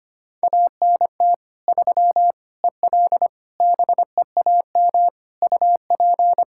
Morse code, heard as a keyed tone, about 25 wpm, 700 Hz. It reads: ANT 3 EL BEAM UP
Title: Practice QSO - Exchange 2 - Info - 25wpm